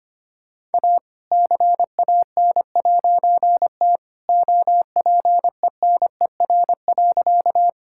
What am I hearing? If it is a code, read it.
Morse code, heard as a keyed tone, about 25 wpm, 700 Hz. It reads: A CAN'T OPENER.
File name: practice / 25wpm / All-Age Appropriate Jokes - 25wpm